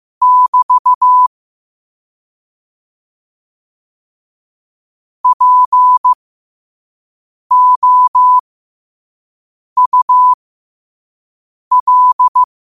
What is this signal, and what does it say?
Morse code, heard as a keyed tone, about 15 wpm, spaced as for 5 wpm, 1000 Hz. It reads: <BT> POUL
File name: qotdfiles / 5wpm